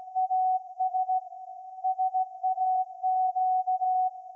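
Morse code, heard as a keyed tone, about 16 wpm, 750 Hz. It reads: AS SAQ
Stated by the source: Morse code from SAQ, received at a station in Putignano, Apulia, Italy, on VLF.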